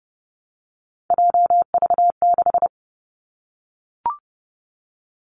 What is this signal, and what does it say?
Morse code, heard as a keyed tone, about 30 wpm, 700 Hz. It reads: JV6
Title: Three Challenging Letter-Number Combinations 30wpm